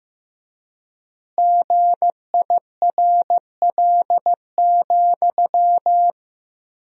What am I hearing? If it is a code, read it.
Morse code, heard as a keyed tone, about 16 wpm, 700 Hz. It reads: GIRL,
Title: 3x Word-Spacing - Sentences from Top 400 Words - 15wpm